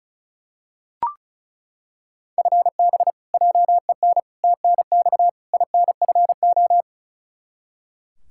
Morse code, heard as a keyed tone, about 36 wpm, 700 Hz. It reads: FB JEN TNX INFO
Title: Practice QSO - Exchange 3 - More Info - 35wpm